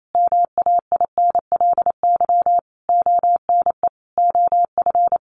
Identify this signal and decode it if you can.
Morse code, heard as a keyed tone, about 28 wpm, 700 Hz. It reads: MAINLY ONE OF
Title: Sentences from Top 1400 Words 28wpm